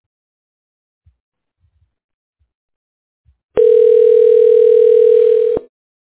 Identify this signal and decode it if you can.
Ringback tone (North America), 440 + 480 Hz.